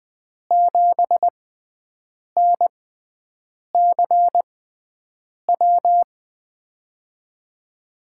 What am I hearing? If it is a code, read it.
Morse code, heard as a keyed tone, about 20 wpm, spaced as for 7 wpm, 700 Hz. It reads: 7NCW
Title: USA Call Signs - 6 Characters - 6x Character Spacing - 20wpm